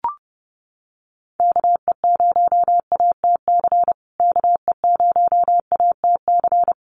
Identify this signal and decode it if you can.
Morse code, heard as a keyed tone, about 30 wpm, 700 Hz. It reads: KE0ATC KE0ATC
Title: USA Call Signs Repeated 3x 30wpm